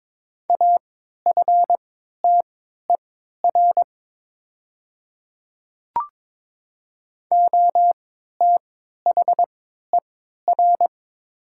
Morse code, heard as a keyed tone, about 22 wpm, spaced as for 13 wpm, 700 Hz. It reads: AFTER OTHER
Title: 3x Character Spacing - Top 200 Words - 22wpm